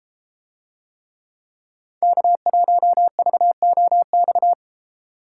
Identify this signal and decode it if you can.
Morse code, heard as a keyed tone, about 32 wpm, 700 Hz. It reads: K1VOX